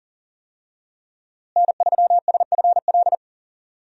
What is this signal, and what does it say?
Morse code, heard as a keyed tone, about 40 wpm, 700 Hz. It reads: N3SFL
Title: USA Call Signs - 5 Characters - Repeated 3x - 40wpm